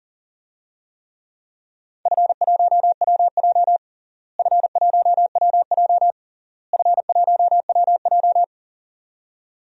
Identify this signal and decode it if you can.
Morse code, heard as a keyed tone, about 40 wpm, 700 Hz. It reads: F1WJ F1WJ F1WJ
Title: DXCC-100 Call Signs - 4 Characters - Repeated 3x - 40wpm